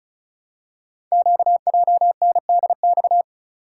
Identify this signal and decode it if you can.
Morse code, heard as a keyed tone, about 36 wpm, 700 Hz. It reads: QJNDX
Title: Code Groups - 5 Letters - 35wpm